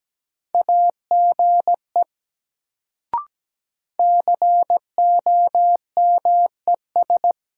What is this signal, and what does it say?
Morse code, heard as a keyed tone, about 17 wpm, 700 Hz. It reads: AGE COMES
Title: Sets of 3 Words - New Word Drill - 17wpm